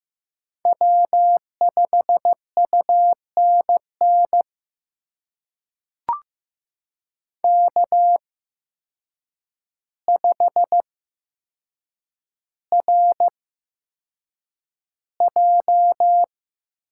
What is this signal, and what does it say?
Morse code, heard as a keyed tone, about 15 wpm, spaced as for 7 wpm, 700 Hz. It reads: W5UNN K5RJ
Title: USA Call Signs - 5 Characters - 8x Character Spacing - 15wpm